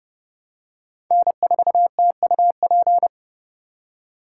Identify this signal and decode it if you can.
Morse code, heard as a keyed tone, about 30 wpm, 700 Hz. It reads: N4TUP